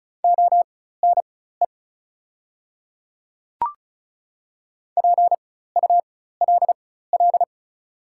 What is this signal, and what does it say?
Morse code, heard as a keyed tone, about 34 wpm, spaced as for 16 wpm, 690 Hz. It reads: ONE PULL
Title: 4x Character Spacing - Top 500 Words - 35wpm